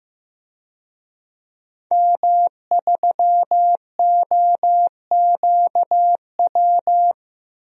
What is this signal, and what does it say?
Morse code, heard as a keyed tone, about 15 wpm, 700 Hz. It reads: M3OQW
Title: DXCC-100 Call Signs - 5 Characters - 15wpm